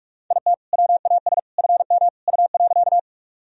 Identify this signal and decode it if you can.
Morse code, heard as a keyed tone, about 44 wpm, 700 Hz. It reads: IT WAS FM U.
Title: Sentences from Top 100 Words and Abbreviations 45wpm